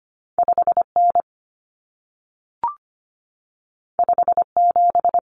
Morse code, heard as a keyed tone, about 25 wpm, 700 Hz. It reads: 5N 57
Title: Common Cut Numbers - 2-Digit 25wpm